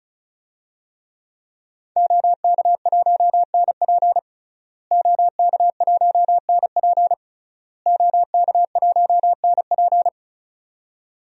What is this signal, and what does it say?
Morse code, heard as a keyed tone, about 36 wpm, 700 Hz. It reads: OK1NP OK1NP OK1NP
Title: DXCC-100 Call Signs - 5 Characters - Repeated 3x - 35wpm